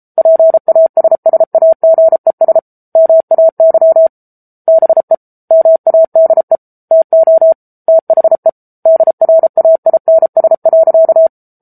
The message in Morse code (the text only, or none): PASSAGES MAY BE MADE TO THE DRAINS.